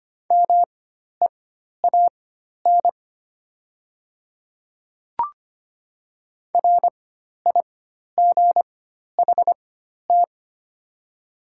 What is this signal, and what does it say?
Morse code, heard as a keyed tone, about 25 wpm, spaced as for 12 wpm, 700 Hz. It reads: MEAN RIGHT
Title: 4x Character Spacing - Top 200 Words - 25wpm